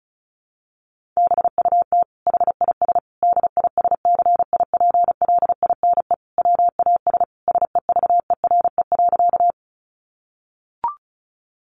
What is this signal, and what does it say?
Morse code, heard as a keyed tone, about 34 wpm, 700 Hz. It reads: BUT HIS DISCIPLINE WAS SEVERE.
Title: Sentences from Top 1400 Words 35wpm